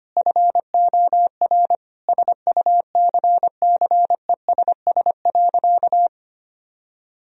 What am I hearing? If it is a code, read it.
Morse code, heard as a keyed tone, about 25 wpm, 700 Hz. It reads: FOR SUCCESS.
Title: Sentences from Top 1000 Words - Review Part 3 - 25wpm